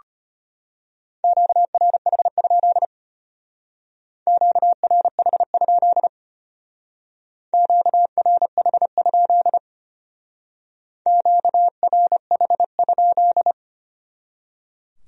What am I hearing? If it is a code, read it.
Morse code, heard as a keyed tone, about 30 wpm, 700 Hz. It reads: QRH? QRH? QRH? QRH?
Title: Speed-Racing Q-Codes - 25wpm